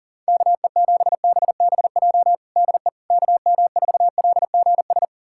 Morse code, heard as a keyed tone, about 37 wpm, 700 Hz. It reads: KE7BBJ DE KM4LGS
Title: USA Callsign DE Callsign 40wpm